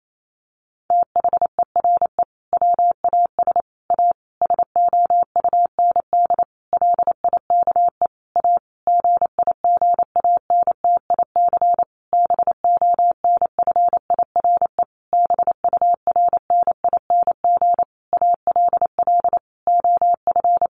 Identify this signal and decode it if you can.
Morse code, heard as a keyed tone, about 28 wpm, 700 Hz. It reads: THERE WAS A SOUND LIKE A GIGANTIC BONFIRE BURNING ALL OF
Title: A Sound Of Thunder 28wpm